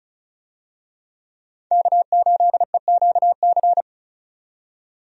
Morse code, heard as a keyed tone, about 36 wpm, 700 Hz. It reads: K8EQC
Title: USA Call Signs - 5 Characters - 35wpm